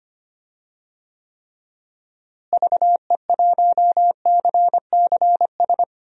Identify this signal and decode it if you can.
Morse code, heard as a keyed tone, about 25 wpm, 700 Hz. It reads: VE1CCS
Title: DXCC-100 Call Signs - 6 Characters - Repeated 2x - 25wpm